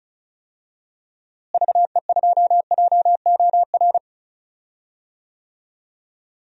Morse code, heard as a keyed tone, about 36 wpm, 700 Hz. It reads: VE2JOR
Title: DXCC-100 Call Signs - 6 Characters - Repeated 2x - 35wpm